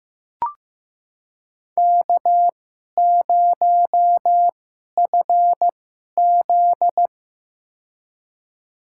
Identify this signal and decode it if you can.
Morse code, heard as a keyed tone, about 15 wpm, spaced as for 11 wpm, 700 Hz. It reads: K0FZ
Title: USA Call Signs - 4 Characters - 2x Character Spacing - 15wpm